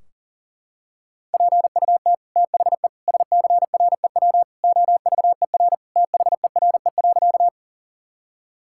Morse code, heard as a keyed tone, about 40 wpm, 700 Hz. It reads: PUT THE SCREW OVER THERE.
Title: Sentences from Top 1600 Words 40wpm